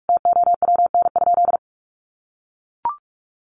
Morse code, heard as a keyed tone, about 44 wpm, 700 Hz. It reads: TOWN?
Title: Sentences from Top 1300 Words 45wpm